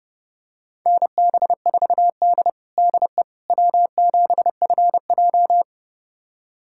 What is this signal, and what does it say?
Morse code, heard as a keyed tone, about 30 wpm, 700 Hz. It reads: NB4D DE W7FJ
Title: USA Callsign DE Callsign - 4 Characters - 30wpm